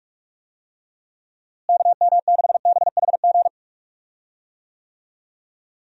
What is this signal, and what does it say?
Morse code, heard as a keyed tone, about 46 wpm, 700 Hz. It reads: KM6BHG
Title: USA Call Signs - 6 Characters - Repeated 3x - 45wpm